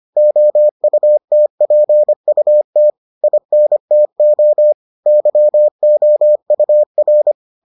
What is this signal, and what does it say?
Morse code, heard as a keyed tone, about 25 wpm, 600 Hz. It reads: OUTPUT INTO YOUR